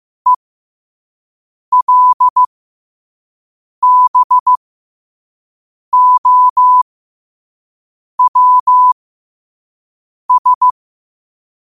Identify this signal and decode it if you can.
Morse code, heard as a keyed tone, about 15 wpm, spaced as for 5 wpm, 1000 Hz. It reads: ELBOWS
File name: qotdfiles / 5wpm